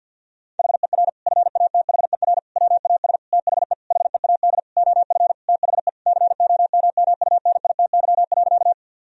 Morse code, heard as a keyed tone, about 50 wpm, 700 Hz. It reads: HER FATHER WAS THE HEAD OF THE COMMUNITY.
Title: Sentences from Top 1000 Words - Review Part 4 - 50wpm